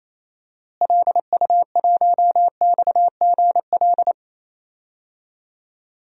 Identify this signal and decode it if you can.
Morse code, heard as a keyed tone, about 28 wpm, 700 Hz. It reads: LU1XGL
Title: DXCC-100 Call Signs - 6 Characters - Repeated 3x - 28wpm